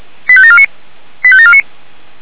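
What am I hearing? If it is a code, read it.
Five-tone selective call (ZVEI), heard as quick runs of five tones, tones of about 70 ms. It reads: 85749 85749
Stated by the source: ZVEI five-tone call 85749 received by SDR.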